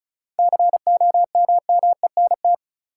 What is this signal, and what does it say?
Morse code, heard as a keyed tone, about 36 wpm, 700 Hz. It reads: COMMENT
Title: Seven Letter Words - 35wpm